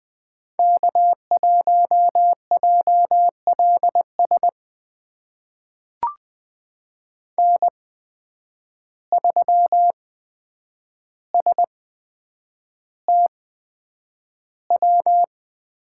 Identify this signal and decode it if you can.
Morse code, heard as a keyed tone, about 20 wpm, spaced as for 9 wpm, 700 Hz. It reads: K1JLS N3STW
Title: USA Call Signs - 5 Characters - 8x Character Spacing - 20wpm